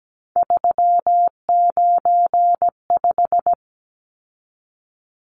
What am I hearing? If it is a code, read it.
Morse code, heard as a keyed tone, about 17 wpm, 700 Hz. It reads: 395
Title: Numbers - 3-Digit 17wpm